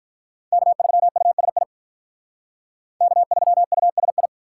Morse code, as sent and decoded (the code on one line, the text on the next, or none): -.- ...-- ..- ... .. / -.- ...-- ..- ... ..
K3USI K3USI